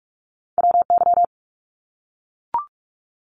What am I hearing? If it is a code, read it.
Morse code, heard as a keyed tone, about 44 wpm, 700 Hz. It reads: WY
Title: ARRL Sections 45wpm